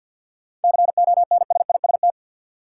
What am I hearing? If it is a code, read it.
Morse code, heard as a keyed tone, about 53 wpm, 700 Hz. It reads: CONSIST
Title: Sentences from Top 1400 Words - New Word Drill - 50wpm